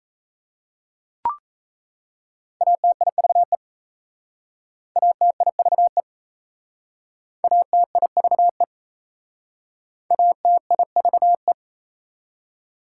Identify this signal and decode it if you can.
Morse code, heard as a keyed tone, about 35 wpm, 690 Hz. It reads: ATIVE ATIVE ATIVE ATIVE